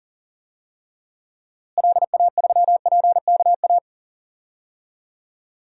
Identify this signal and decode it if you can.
Morse code, heard as a keyed tone, about 40 wpm, 700 Hz. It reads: LA3PKA
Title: DXCC-100 Call Signs - 6 Characters - Repeated 2x - 40wpm